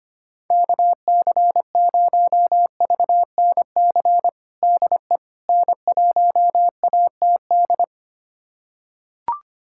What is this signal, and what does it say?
Morse code, heard as a keyed tone, about 25 wpm, 700 Hz. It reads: KC0VNC DE N1ATD
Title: USA Callsign DE Callsign 25wpm